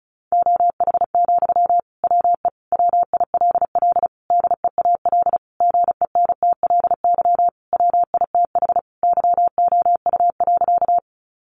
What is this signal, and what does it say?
Morse code, heard as a keyed tone, about 34 wpm, 700 Hz. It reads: OH, WE WILL DEAL GENTLY WITH YOU.
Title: Sentences from Top 1500 Words 35wpm